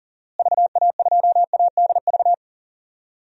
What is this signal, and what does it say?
Morse code, heard as a keyed tone, about 37 wpm, 700 Hz. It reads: VA2ADV
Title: DXCC-100 Call Signs - 6 Characters - Repeated 3x - 40wpm